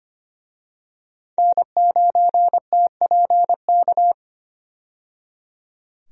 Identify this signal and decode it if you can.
Morse code, heard as a keyed tone, about 25 wpm, 700 Hz. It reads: N9TPK
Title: USA Call Signs - 5 Characters - 25wpm